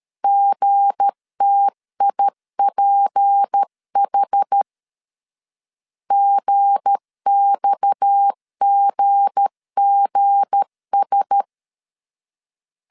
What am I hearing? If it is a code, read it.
Morse code, heard as a keyed tone, about 13 wpm, 800 Hz. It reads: GTIPH GXGGS
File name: LEZIONI / 7 GF9Z / 7 G 12wpm 2055